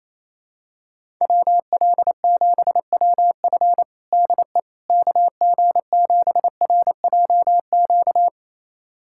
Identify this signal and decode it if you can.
Morse code, heard as a keyed tone, about 28 wpm, 700 Hz. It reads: WL7WF DE KG7RJQ
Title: USA Callsign DE Callsign 28wpm